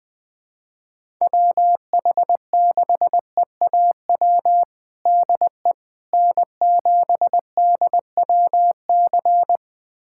Morse code, heard as a keyed tone, about 20 wpm, 700 Hz. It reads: WH6EAW DE N7DWC
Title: USA Callsign DE Callsign 20wpm